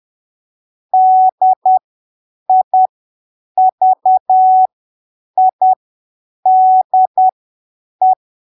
Morse code, heard as a keyed tone, about 10 wpm, spaced as for 7 wpm, 750 Hz. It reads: DIVIDE
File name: arquivos / texto / texto 29 10ppm